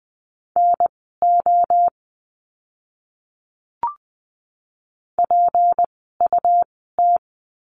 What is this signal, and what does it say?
Morse code, heard as a keyed tone, about 20 wpm, spaced as for 15 wpm, 700 Hz. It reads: NO PUT
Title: 192 Lesson 31f - Letter v - Words 20wpm